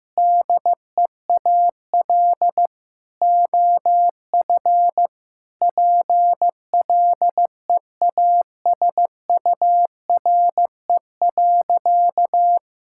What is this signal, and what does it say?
Morse code, heard as a keyed tone, about 16 wpm, 700 Hz. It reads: DEAL OF PLEASURE.
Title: Sentences from Top 1600 Words 15wpm